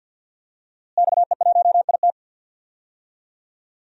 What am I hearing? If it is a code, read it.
Morse code, heard as a keyed tone, about 46 wpm, 700 Hz. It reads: XE1IT
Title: DXCC-100 Call Signs - 5 Characters - Repeated 2x - 50wpm